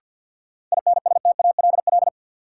Morse code, heard as a keyed tone, about 52 wpm, 700 Hz. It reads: INSTALL
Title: Sentences from Top 1600 Words - New Word Drill - 50wpm